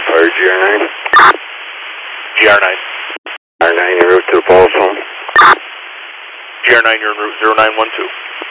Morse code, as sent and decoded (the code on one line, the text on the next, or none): none